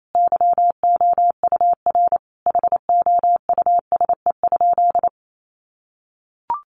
Morse code, heard as a keyed tone, about 28 wpm, 700 Hz. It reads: YOUR HOUSE?
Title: Sentences from Top 1500 Words 28wpm